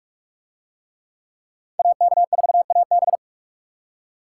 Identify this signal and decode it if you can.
Morse code, heard as a keyed tone, about 43 wpm, 700 Hz. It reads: AK4AB